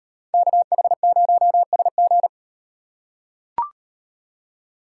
Morse code, heard as a keyed tone, about 38 wpm, 700 Hz. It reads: KH0SG